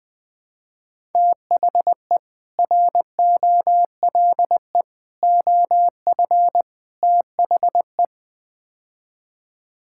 Morse code, heard as a keyed tone, about 20 wpm, 700 Hz. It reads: THE ROLE OF THE
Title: Build A Phrase - 4-Word Phrases - 20wpm